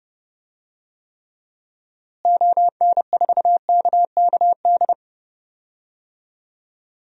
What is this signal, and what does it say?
Morse code, heard as a keyed tone, about 30 wpm, 700 Hz. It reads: ON4KKD
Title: DXCC-100 Call Signs - 6 Characters - Repeated 2x - 30wpm